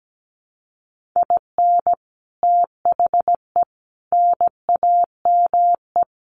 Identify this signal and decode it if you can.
Morse code, heard as a keyed tone, about 17 wpm, 700 Hz. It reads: IN THE NAME
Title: Sets of 4 Words 17wpm